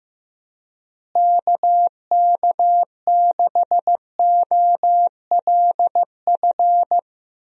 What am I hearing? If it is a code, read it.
Morse code, heard as a keyed tone, about 15 wpm, 700 Hz. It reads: KK6OLF
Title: USA Call Signs - 6 Characters - Repeated 3x - 15wpm